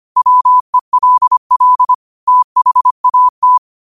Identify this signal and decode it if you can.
Morse code, heard as a keyed tone, about 25 wpm, 1000 Hz. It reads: WELL THAT
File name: qotdfiles / 25wpm